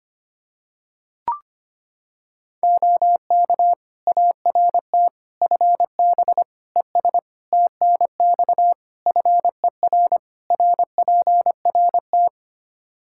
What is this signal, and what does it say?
Morse code, heard as a keyed tone, about 25 wpm, 700 Hz. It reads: OK ART FB ES TNX FER RPRT
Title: Practice QSO - Exchange 2 - Info - 25wpm